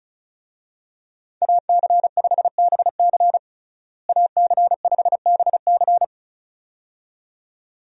Morse code, heard as a keyed tone, about 36 wpm, 700 Hz. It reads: AC5BC AC5BC
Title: USA Call Signs - 5 Characters - Repeated 2x - 35wpm